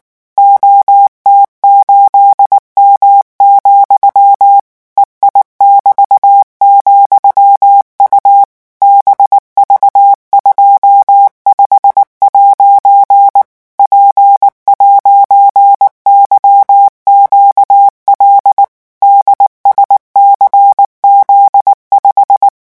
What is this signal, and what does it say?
Morse code, heard as a keyed tone, about 19 wpm, 800 Hz. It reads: OT8M, EI<BT>,U BV25' P'YQL DSCZ5
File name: morse mp3 / revision13-vitesse 18 mots